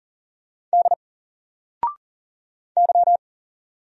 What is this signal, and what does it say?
Morse code, heard as a keyed tone, about 40 wpm, 700 Hz. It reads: D Y
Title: Single Letter-Number 40wpm